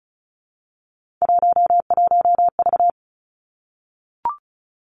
Morse code, heard as a keyed tone, about 35 wpm, 700 Hz. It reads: 11V
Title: Three Challenging Letter-Number Combinations 35wpm